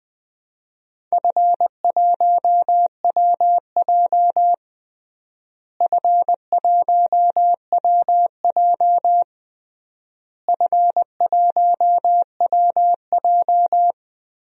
Morse code, heard as a keyed tone, about 21 wpm, 700 Hz. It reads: F1WJ F1WJ F1WJ
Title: DXCC-100 Call Signs - 4 Characters - Repeated 3x - 20wpm